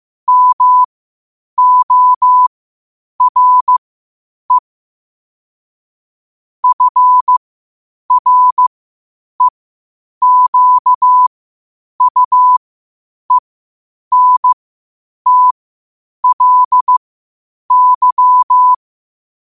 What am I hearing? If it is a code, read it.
Morse code, heard as a keyed tone, about 15 wpm, spaced as for 8 wpm, 1000 Hz. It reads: MORE FREQUENTLY
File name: qotdfiles / 8wpm